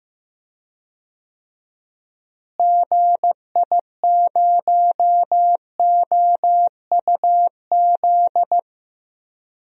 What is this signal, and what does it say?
Morse code, heard as a keyed tone, about 15 wpm, 700 Hz. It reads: GI0OUZ
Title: DXCC-100 Call Signs - 6 Characters - Repeated 3x - 15wpm